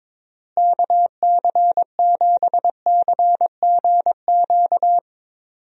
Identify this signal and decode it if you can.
Morse code, heard as a keyed tone, about 21 wpm, 700 Hz. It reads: KC7CGQ
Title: USA Call Signs - 6 Characters - Repeated 2x - 22wpm